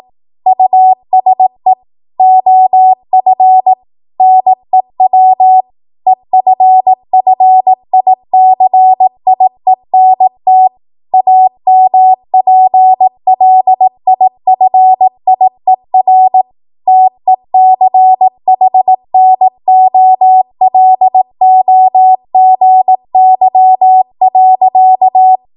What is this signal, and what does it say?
Morse code, heard as a keyed tone, about 18 wpm, 750 Hz. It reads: USE OF NEW EFFICIENT AMPLIFIER TECHNOLOGY.
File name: Morse / 18WPM